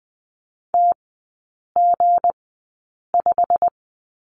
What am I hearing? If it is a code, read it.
Morse code, heard as a keyed tone, about 20 wpm, spaced as for 8 wpm, 700 Hz. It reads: TG5
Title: 159 Lesson 26e - Letter g - Three Characters 20wpm